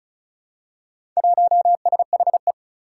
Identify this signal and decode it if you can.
Morse code, heard as a keyed tone, about 34 wpm, 700 Hz. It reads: 1SHE
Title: Four Challenging Letter-Number Combinations 35wpm